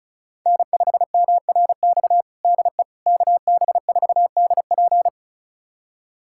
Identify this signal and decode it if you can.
Morse code, heard as a keyed tone, about 36 wpm, 690 Hz. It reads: N5MRX DE KB4DP
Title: USA Callsign DE Callsign - 5 Characters - 35wpm